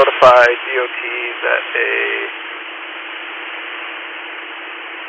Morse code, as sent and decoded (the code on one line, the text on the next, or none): none